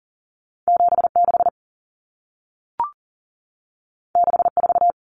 Morse code, heard as a keyed tone, about 40 wpm, 700 Hz. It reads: 76 64
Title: Numbers - 2-Digit 40wpm